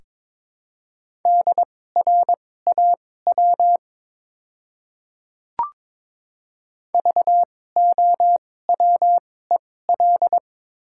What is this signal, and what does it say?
Morse code, heard as a keyed tone, about 22 wpm, spaced as for 16 wpm, 700 Hz. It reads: DRAW VOWEL
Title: 2x Character Spacing - Top 500 Words - 22wpm